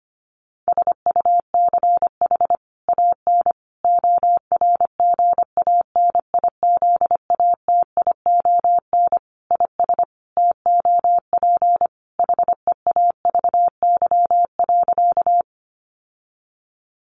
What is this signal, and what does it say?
Morse code, heard as a keyed tone, about 25 wpm, 700 Hz. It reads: SUCH AN ORGANIZATION IS TOP HEAVY.
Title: Sentences from Top 1100 Words 25wpm